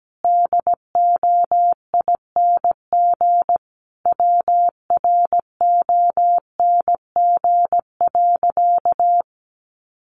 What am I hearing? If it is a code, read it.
Morse code, heard as a keyed tone, about 17 wpm, 700 Hz. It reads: DOING WRONG.
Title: Easy Phrases 17wpm